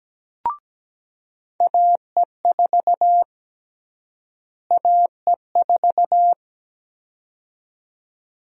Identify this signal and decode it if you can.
Morse code, heard as a keyed tone, about 17 wpm, 700 Hz. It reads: AE4 AE4
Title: USA Call Sign Prefixes - Repeated 2x - 17wpm